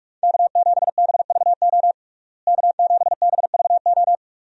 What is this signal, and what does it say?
Morse code, heard as a keyed tone, about 46 wpm, 700 Hz. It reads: K7BVO K7BVO
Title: USA Call Signs - 5 Characters - Repeated 2x - 45wpm